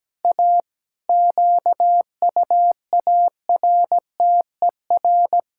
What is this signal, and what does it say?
Morse code, heard as a keyed tone, about 17 wpm, 700 Hz. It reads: A QUARTER
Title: Pangram Challenge v2 - 17wpm